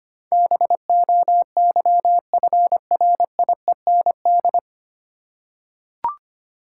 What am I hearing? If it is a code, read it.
Morse code, heard as a keyed tone, about 25 wpm, 700 Hz. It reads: BOYFRIEND
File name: Nine Letter Words - 25wpm